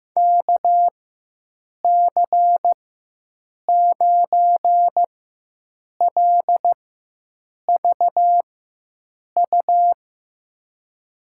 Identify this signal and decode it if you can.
Morse code, heard as a keyed tone, about 15 wpm, spaced as for 7 wpm, 700 Hz. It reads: KC9LVU
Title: USA Call Signs - 4x Character Spacing - 15wpm